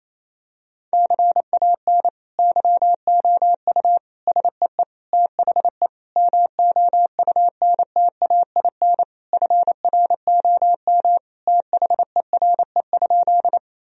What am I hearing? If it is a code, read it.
Morse code, heard as a keyed tone, about 28 wpm, 700 Hz. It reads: CAN YOU SEE THE MOUNTAIN FROM THERE?
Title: Sentences from Top 1000 Words - Review Part 1 - 28wpm